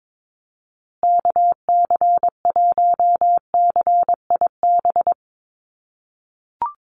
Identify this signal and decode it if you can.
Morse code, heard as a keyed tone, about 22 wpm, 700 Hz. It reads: KC1CIB
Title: USA Call Signs Repeated 3x 22wpm